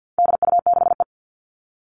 Morse code, heard as a keyed tone, about 50 wpm, 700 Hz. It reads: DU6E
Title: All Cut Numbers - 4-Digit 50wpm